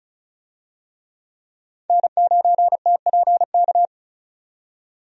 Morse code, heard as a keyed tone, about 36 wpm, 700 Hz. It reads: N9TPK
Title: USA Call Signs - 5 Characters - 35wpm